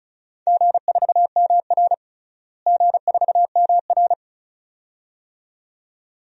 Morse code, heard as a keyed tone, about 36 wpm, 700 Hz. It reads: G4MR G4MR
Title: DXCC-100 Call Signs - 4 Characters - Repeated 2x - 35wpm